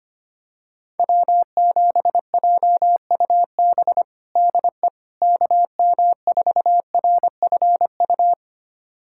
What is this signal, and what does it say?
Morse code, heard as a keyed tone, about 25 wpm, 700 Hz. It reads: W7JUB DE KM4RFU
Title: USA Callsign DE Callsign 25wpm